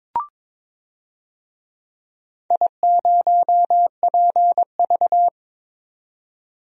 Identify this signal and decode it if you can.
Morse code, heard as a keyed tone, about 22 wpm, 700 Hz. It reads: I0PV
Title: DXCC-100 Call Signs - 4 Characters - 22wpm